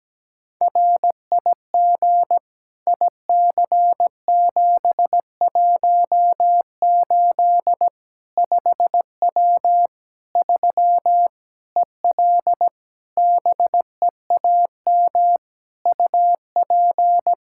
Morse code, heard as a keyed tone, about 17 wpm, 700 Hz. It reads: RIG IC718 5W 3 EL BEAM UP
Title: K1USN SST QSO Practice 17wpm